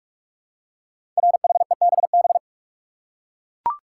Morse code, heard as a keyed tone, about 44 wpm, 700 Hz. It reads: RHEBB
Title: Code Groups - 5 Letters - 45wpm